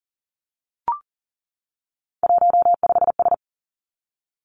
Morse code, heard as a keyed tone, about 42 wpm, 700 Hz. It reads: 15S
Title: Three Challenging Letter-Number Combinations 40wpm